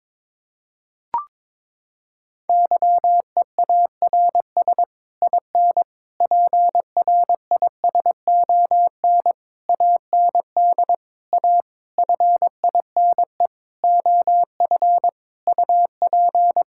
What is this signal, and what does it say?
Morse code, heard as a keyed tone, about 22 wpm, 700 Hz. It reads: YEARS IN PRISON AND A FINE OF UP
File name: Sets of 9 Words - 22wpm